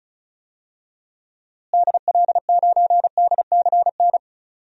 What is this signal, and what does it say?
Morse code, heard as a keyed tone, about 36 wpm, 700 Hz. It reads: DL9DCN
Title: DXCC-100 Call Signs - 6 Characters - Repeated 3x - 35wpm